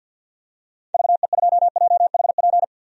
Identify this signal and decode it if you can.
Morse code, heard as a keyed tone, about 50 wpm, 700 Hz. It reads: VE2JHP
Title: DXCC-100 Call Signs - 6 Characters - 50wpm